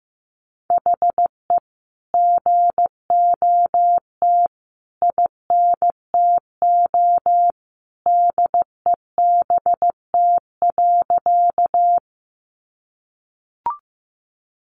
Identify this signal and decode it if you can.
Morse code, heard as a keyed tone, about 15 wpm, 700 Hz. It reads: HE GOT INTO DEBT.
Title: Sentences from Top 1300 Words 15wpm